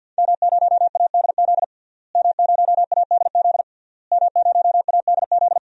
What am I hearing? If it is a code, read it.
Morse code, heard as a keyed tone, about 47 wpm, 700 Hz. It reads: M0ADZ M0ADZ M0ADZ
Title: DXCC-100 Call Signs - 5 Characters - Repeated 3x - 50wpm